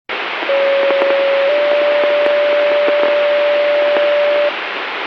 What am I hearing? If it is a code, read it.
Two-tone page, heard as a one-second tone, then a three-second tone, about 550 then 580 Hz, with noise. It is heard from 0.5 to 4.5 s.